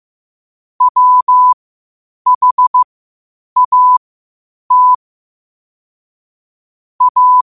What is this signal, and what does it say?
Morse code, heard as a keyed tone, about 15 wpm, spaced as for 8 wpm, 1000 Hz. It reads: WHAT A
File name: qotdfiles / 8wpm